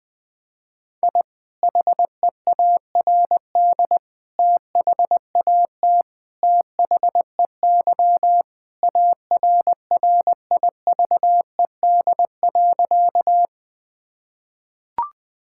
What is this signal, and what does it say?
Morse code, heard as a keyed tone, about 21 wpm, 700 Hz. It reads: I HEARD THAT THEY ARRIVED.
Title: Sentences with 2-Word Phrases - 20wpm